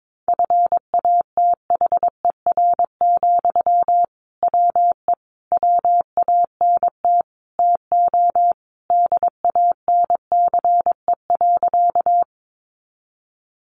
Morse code, heard as a keyed tone, about 22 wpm, 700 Hz. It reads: FATHER, WE WANT TO DANCE.
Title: Sentences from Top 700 Words 22wpm